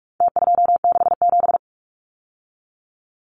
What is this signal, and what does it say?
Morse code, heard as a keyed tone, about 44 wpm, 700 Hz. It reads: T267
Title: Numbers - Contest Style - 4-Digit 45wpm